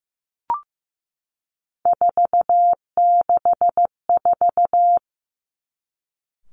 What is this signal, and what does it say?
Morse code, heard as a keyed tone, about 15 wpm, 700 Hz. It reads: 464
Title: Three Challenging Letter-Number Combinations 15wpm